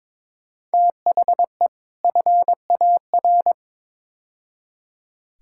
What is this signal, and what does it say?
Morse code, heard as a keyed tone, about 21 wpm, 700 Hz. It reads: THE FAR
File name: practice / 22wpm / Build A Phrase - 5-Word Phrases - 22wpm